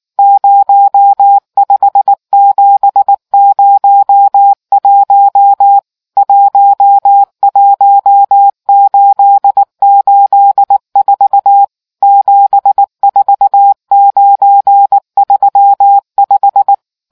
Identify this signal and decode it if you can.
Morse code, heard as a keyed tone, about 19 wpm, 800 Hz. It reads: 05701 11884 74935